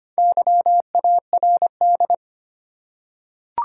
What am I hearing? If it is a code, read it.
Morse code, heard as a keyed tone, about 25 wpm, 700 Hz. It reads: YARD
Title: Four Letter Words - 25wpm